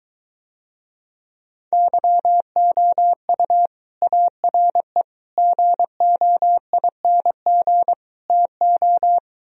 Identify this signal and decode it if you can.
Morse code, heard as a keyed tone, about 24 wpm, 700 Hz. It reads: YOU ARE GOING TO